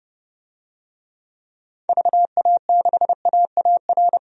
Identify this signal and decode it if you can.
Morse code, heard as a keyed tone, about 30 wpm, 700 Hz. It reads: VA6AAR
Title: DXCC-100 Call Signs - 6 Characters - 30wpm